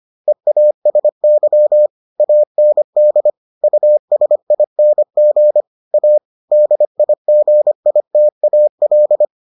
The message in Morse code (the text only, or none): EASY AND USING A DIGITAL